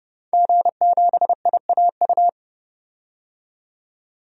Morse code, heard as a keyed tone, about 30 wpm, 700 Hz. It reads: G7IAU
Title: DXCC-100 Call Signs - 5 Characters - 30wpm